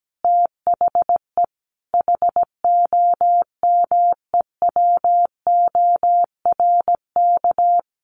Morse code, heard as a keyed tone, about 17 wpm, 700 Hz. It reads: THE HOMEWORK
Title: Easy Phrases 17wpm